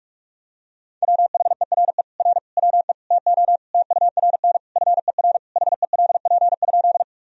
Morse code, heard as a keyed tone, about 46 wpm, 700 Hz. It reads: WHERE R WE TO TURN FER HELP?
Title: Sentences from Top 200 Words and Abbreviations 45wpm